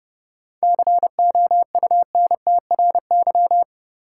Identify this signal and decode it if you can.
Morse code, heard as a keyed tone, about 30 wpm, 700 Hz. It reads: COUNTRY
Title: Sentences from Top 300 Words - New Word Drill - 30wpm